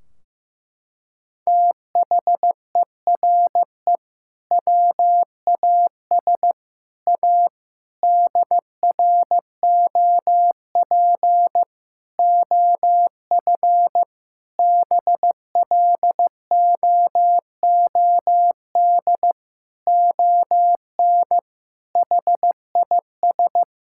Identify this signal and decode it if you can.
Morse code, heard as a keyed tone, about 15 wpm, 700 Hz. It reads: THERE WAS A DROP OF BLOOD ON HIS
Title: Sentences from Top 1600 Words 15wpm